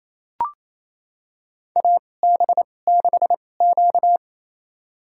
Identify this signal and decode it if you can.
Morse code, heard as a keyed tone, about 28 wpm, spaced as for 20 wpm, 700 Hz. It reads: AB6Q
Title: USA Call Signs - 4 Characters - 2x Character Spacing - 28wpm